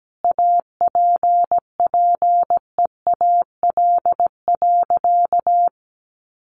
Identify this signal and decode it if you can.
Morse code, heard as a keyed tone, about 17 wpm, 700 Hz. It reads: APPEAL.
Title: Sentences from Top 1300 Words 17wpm